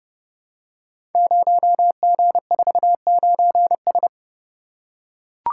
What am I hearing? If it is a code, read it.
Morse code, heard as a keyed tone, about 30 wpm, 700 Hz. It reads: 0G49S
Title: Code Groups - 5 Letter:Numbers - 30wpm